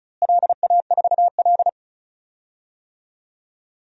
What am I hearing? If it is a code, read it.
Morse code, heard as a keyed tone, about 36 wpm, 700 Hz. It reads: LA4L
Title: DXCC-100 Call Signs - 4 Characters - Repeated 3x - 35wpm